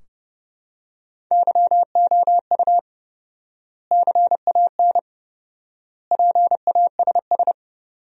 Morse code, heard as a keyed tone, about 32 wpm, 700 Hz. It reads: YOU CAN PASS
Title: 4x Word-Spacing - Sentences from Top 400 Words - 30wpm